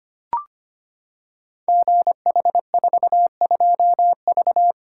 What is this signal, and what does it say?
Morse code, heard as a keyed tone, about 25 wpm, 700 Hz. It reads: GH42V
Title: Code Groups - 5 Letter:Numbers - 25wpm